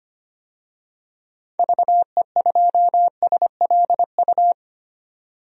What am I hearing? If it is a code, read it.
Morse code, heard as a keyed tone, about 25 wpm, 700 Hz. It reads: VE2SLU
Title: DXCC-100 Call Signs - 6 Characters - 25wpm